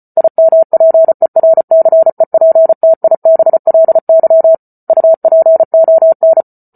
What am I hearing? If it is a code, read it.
Morse code, heard as a keyed tone, about 34 wpm, 650 Hz. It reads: IMPERCEPTIBLY UPON